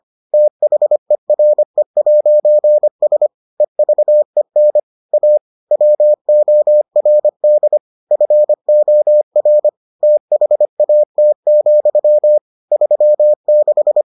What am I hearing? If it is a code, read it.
Morse code, heard as a keyed tone, about 25 wpm, 600 Hz. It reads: THERE'S EVEN A WORD FOR THAT, 36